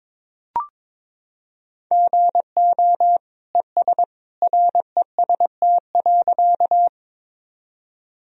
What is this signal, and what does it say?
Morse code, heard as a keyed tone, about 21 wpm, 700 Hz. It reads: GO ES REST.
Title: Sentences from Top 500 Words and Abbreviations 22wpm